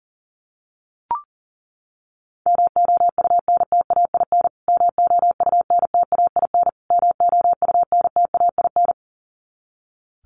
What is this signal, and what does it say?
Morse code, heard as a keyed tone, about 40 wpm, 700 Hz. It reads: MOUNTAIN MOUNTAIN MOUNTAIN
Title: Top 500 Words Repeated 40wpm